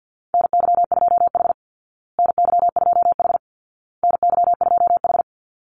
Morse code, heard as a keyed tone, about 50 wpm, 700 Hz. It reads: NY2H NY2H NY2H
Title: Call Signs Repeated 2x - 50wpm